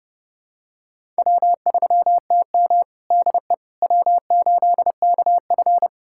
Morse code, heard as a keyed tone, about 30 wpm, 700 Hz. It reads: W3TM DE W8KF
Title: USA Callsign DE Callsign - 4 Characters - 30wpm